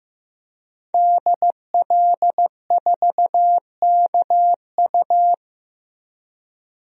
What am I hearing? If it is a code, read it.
Morse code, heard as a keyed tone, about 15 wpm, 700 Hz. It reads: DL4KU
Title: DXCC-100 Call Signs - 5 Characters - 15wpm